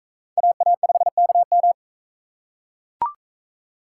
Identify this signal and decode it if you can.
Morse code, heard as a keyed tone, about 42 wpm, 700 Hz. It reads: AA5KM